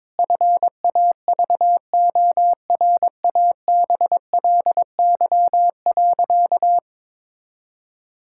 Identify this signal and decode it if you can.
Morse code, heard as a keyed tone, about 22 wpm, 700 Hz. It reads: FAVORABLY.
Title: Pangram Challenge v2 - 22wpm